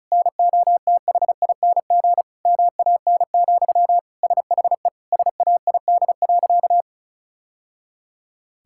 Morse code, heard as a keyed tone, about 36 wpm, 700 Hz. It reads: NOTHING MAN, SHE SAID.
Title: Sentences from Top 1000 Words - Review Part 6 - 35wpm